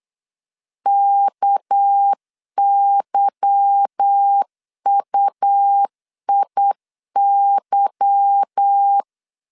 Morse code, heard as a keyed tone, about 8 wpm, 800 Hz. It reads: KYUIY
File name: LEZIONI / 8 YJ45 / 8 Y 8wpm 3245